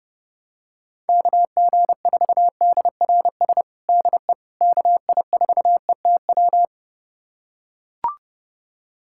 Morse code, heard as a keyed tone, about 32 wpm, 700 Hz. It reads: KG4DRS DE KI4ETW
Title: USA Callsign DE Callsign - 6 Characters - 30wpm